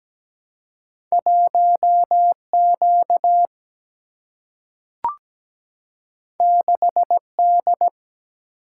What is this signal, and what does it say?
Morse code, heard as a keyed tone, about 17 wpm, 700 Hz. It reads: 1 Q 6 D
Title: Single-Letter Single-Number 17wpm